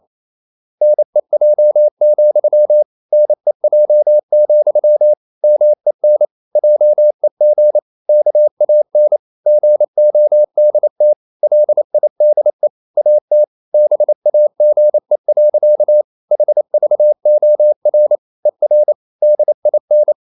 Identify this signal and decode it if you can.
Morse code, heard as a keyed tone, about 28 wpm, 600 Hz. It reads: NEJ, NEJ, MEN JEG KAN GODT LIDE AT BAGE. HVOR ER DIN
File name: Ferie QRQ 28wpm